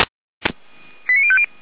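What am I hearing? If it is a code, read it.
Five-tone selective call (ZVEI), heard as quick runs of five tones, tones of about 70 ms. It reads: 89050
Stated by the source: ZVEI five-tone selective call 89140, received by SDR, Feuerwehr Orlamünde fire brigade.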